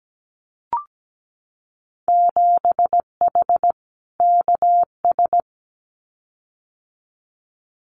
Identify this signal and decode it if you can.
Morse code, heard as a keyed tone, about 17 wpm, 700 Hz. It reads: 7H KS
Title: Winter Field Day Exchanges Encore 17wpm